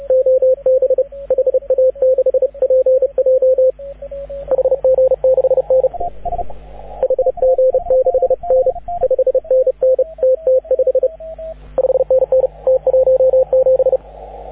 Morse code, heard as a keyed tone, about 34 wpm, 530 Hz, with radio noise. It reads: OB HA6PJ HG6N HG6N 5NN TT5 5NN T17
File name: AUDIO2019 / HG6N CW SSB MIX